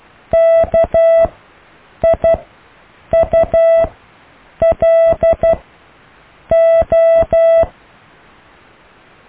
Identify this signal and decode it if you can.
Morse code, heard as a keyed tone, about 12 wpm, spaced as for 7 wpm, 660 Hz. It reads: KIULO
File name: cw / Alle letters